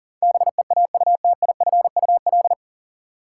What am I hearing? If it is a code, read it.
Morse code, heard as a keyed tone, about 40 wpm, 700 Hz. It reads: BEAUTIFUL
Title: Sentences from Top 900 Words - New Word Drill - 40wpm